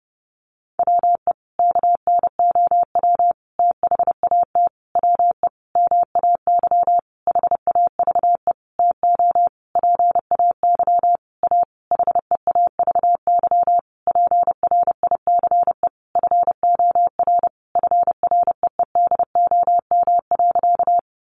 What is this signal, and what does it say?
Morse code, heard as a keyed tone, about 28 wpm, 700 Hz. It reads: WE KNOW THAT WE MAY HAVE TO PAY A HEAVY PRICE FOR FREEDOM.
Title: Sentences from Top 1100 Words 30wpm